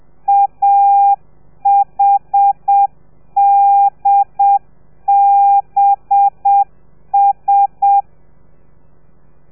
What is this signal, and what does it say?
Morse code, heard as a keyed tone, about 7 wpm, 800 Hz. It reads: AHDBS
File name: telegrafia / i2tfj / Lez4 30